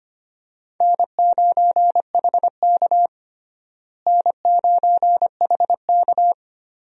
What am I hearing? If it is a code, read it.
Morse code, heard as a keyed tone, about 25 wpm, 700 Hz. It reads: N9HK N9HK
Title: USA Call Signs - 4 Characters - Repeated 3x - 25wpm